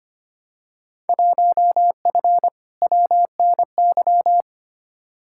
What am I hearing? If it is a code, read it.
Morse code, heard as a keyed tone, about 25 wpm, 700 Hz. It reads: 1F WNY
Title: Field Day Exchanges - 25wpm